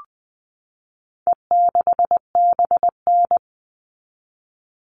Morse code, heard as a keyed tone, about 20 wpm, 700 Hz. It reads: E6BN
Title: All Cut Numbers - 4-Digit 20wpm